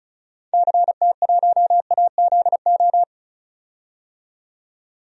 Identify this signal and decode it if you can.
Morse code, heard as a keyed tone, about 36 wpm, 700 Hz. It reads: CT1AZO
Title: DXCC-100 Call Signs - 6 Characters - 35wpm